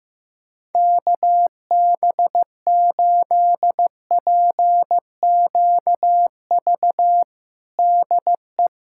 Morse code, heard as a keyed tone, about 15 wpm, 700 Hz. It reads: KB8PQV DE
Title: USA Callsign DE Callsign - 6 Characters - 15wpm